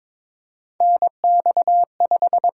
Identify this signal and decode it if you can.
Morse code, heard as a keyed tone, about 22 wpm, 700 Hz. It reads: NX5
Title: USA Call Signs - 4 Characters - 8x Character Spacing - 22wpm